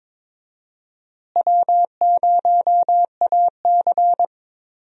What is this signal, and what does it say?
Morse code, heard as a keyed tone, about 22 wpm, 700 Hz. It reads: W0AC
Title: USA Call Signs - 4 Characters - Repeated 2x - 22wpm